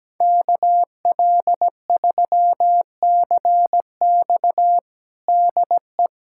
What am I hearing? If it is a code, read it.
Morse code, heard as a keyed tone, about 17 wpm, 700 Hz. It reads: KL3CX DE
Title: USA Callsign DE Callsign 17wpm